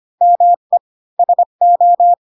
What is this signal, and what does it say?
Morse code, heard as a keyed tone, about 25 wpm, 700 Hz. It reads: ME SO
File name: Pinocchio / Pinocchio ch1 20 25 ik1hge